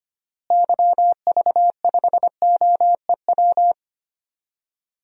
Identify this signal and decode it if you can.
Morse code, heard as a keyed tone, about 25 wpm, 700 Hz. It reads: YV5OEW
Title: DXCC-100 Call Signs - 6 Characters - Repeated 2x - 25wpm